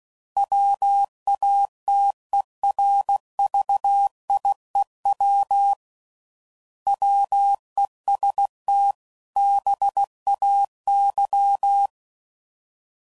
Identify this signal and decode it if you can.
Morse code, heard as a keyed tone, about 16 wpm, 800 Hz. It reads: WATERVIEW WEST BAY
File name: CW Code Groups @15WPM / fromSM's / Cities